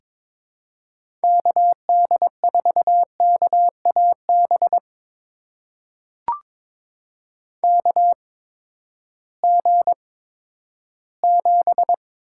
Morse code, heard as a keyed tone, about 21 wpm, spaced as for 12 wpm, 700 Hz. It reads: KD4KAB KG7
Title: USA Call Signs - 6 Characters - 8x Character Spacing - 22wpm